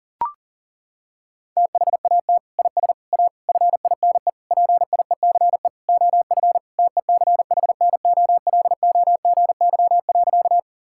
Morse code, heard as a keyed tone, about 40 wpm, 700 Hz. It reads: THAT IS A FINE PIECE OF TECHNOLOGY.
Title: Sentences from Top 1000 Words - Review Part 5 - 40wpm